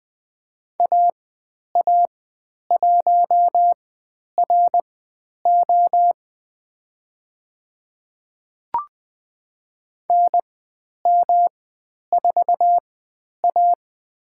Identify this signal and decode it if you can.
Morse code, heard as a keyed tone, about 20 wpm, spaced as for 10 wpm, 700 Hz. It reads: AA1RO NM4A
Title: 109 Lesson 18g - Letter m - Call Signs 20wpm